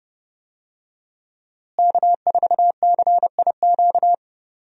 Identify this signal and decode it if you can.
Morse code, heard as a keyed tone, about 30 wpm, 700 Hz. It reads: K4CIQ
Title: USA Call Signs - 5 Characters - Repeated 3x - 30wpm